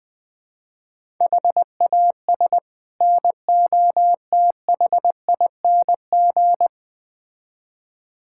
Morse code, heard as a keyed tone, about 20 wpm, 690 Hz. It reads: HAS NOTHING
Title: Build A Phrase - 5-Word Phrases - 20wpm